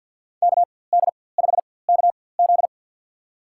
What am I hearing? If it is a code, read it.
Morse code, heard as a keyed tone, about 53 wpm, spaced as for 24 wpm, 700 Hz. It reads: KD5KZ
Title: USA Call Signs - 4x Character Spacing - 50wpm